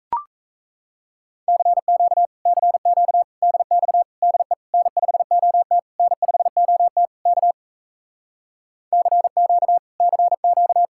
Morse code, heard as a keyed tone, about 37 wpm, 700 Hz. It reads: CQ CQ DX DE N5OT N5OT K CQ CQ